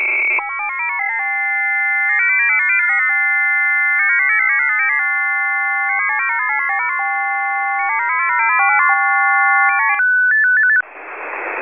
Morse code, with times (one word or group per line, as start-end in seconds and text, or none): none